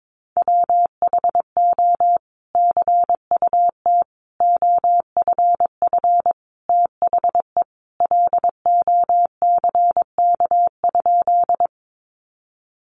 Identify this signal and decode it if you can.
Morse code, heard as a keyed tone, about 22 wpm, 700 Hz. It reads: WHO CUT OFF THE LOCK?
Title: Sentences from Top 1300 Words 22wpm